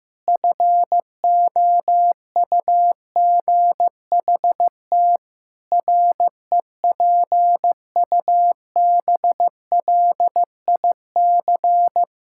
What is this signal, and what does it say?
Morse code, heard as a keyed tone, about 15 wpm, 700 Hz. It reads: FOUGHT REPUBLIC
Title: Pangram Challenge v2 - 15wpm